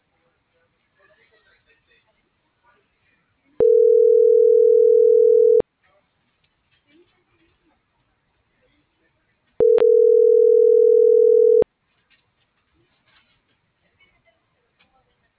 Ringback tone (North America), 440 + 480 Hz.